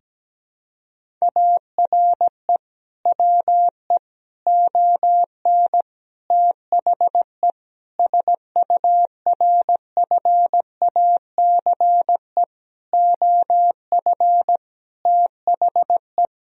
Morse code, heard as a keyed tone, about 17 wpm, 700 Hz. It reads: ARE WE ON THE SURFACE OF THE
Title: Sentences from Top 1000 Words - Review Part 3 - 17wpm